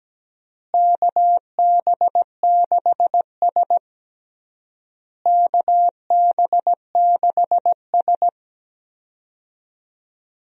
Morse code, heard as a keyed tone, about 17 wpm, 700 Hz. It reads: KB6S KB6S
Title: USA Call Signs - 4 Characters - Repeated 3x - 17wpm